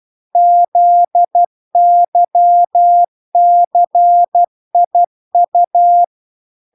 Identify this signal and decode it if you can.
Morse code, heard as a keyed tone, about 12 wpm, 700 Hz. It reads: ZYCIU